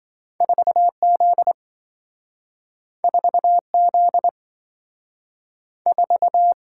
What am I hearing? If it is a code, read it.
Morse code, heard as a keyed tone, about 27 wpm, 700 Hz. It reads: 4Z 4Z 4